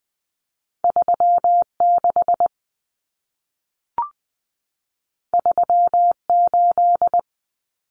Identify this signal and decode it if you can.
Morse code, heard as a keyed tone, about 20 wpm, 700 Hz. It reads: 36 38
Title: Common Cut Numbers - 2-Digit 20wpm